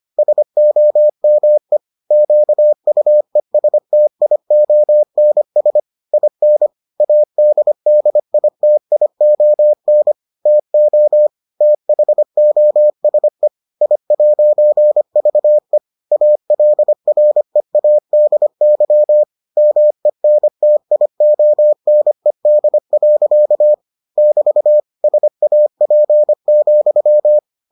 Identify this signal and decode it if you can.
Morse code, heard as a keyed tone, about 25 wpm, 600 Hz. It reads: SOME QUESTIONS IN ADDITION TO THOSE I'VE ALREADY MENTIONED. <BT> SAP,